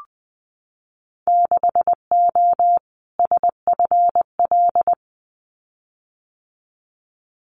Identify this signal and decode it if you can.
Morse code, heard as a keyed tone, about 20 wpm, 700 Hz. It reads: 6O SFL
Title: Winter Field Day Exchanges Encore 20wpm